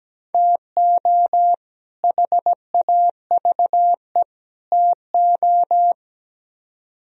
Morse code, heard as a keyed tone, about 20 wpm, 700 Hz. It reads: TO HAVE TO
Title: Speed-Racing - Sets of 3 Words 17wpm